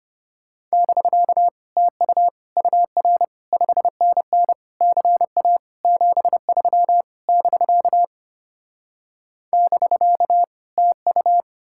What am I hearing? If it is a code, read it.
Morse code, heard as a keyed tone, about 33 wpm, 700 Hz. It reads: <BK> TU UR 5NN CA 73 <BK> <BK> TU